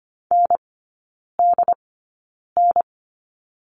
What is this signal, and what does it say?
Morse code, heard as a keyed tone, about 25 wpm, spaced as for 9 wpm, 700 Hz. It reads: NDN
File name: Lesson 17e - Letter u - Three Characters 25wpm